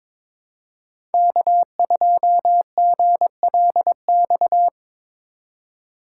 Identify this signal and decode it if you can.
Morse code, heard as a keyed tone, about 22 wpm, 700 Hz. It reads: K2GLX
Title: USA Call Signs - 5 Characters - 3x Character Spacing - 22wpm